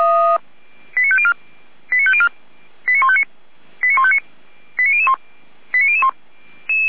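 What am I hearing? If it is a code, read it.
Five-tone selective call (ZVEI), heard as quick runs of five tones, tones of about 80 ms. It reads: 89504 89504 89159 89159 89001 89001
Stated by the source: ZVEI five-tone selective call 89160 received with an SDR, from feuerwehr-orlamuende (Orlamünde fire brigade).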